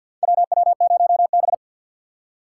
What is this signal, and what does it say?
Morse code, heard as a keyed tone, about 47 wpm, 700 Hz. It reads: WW0B
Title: USA Call Signs - 4 Characters - 3x Character Spacing - 50wpm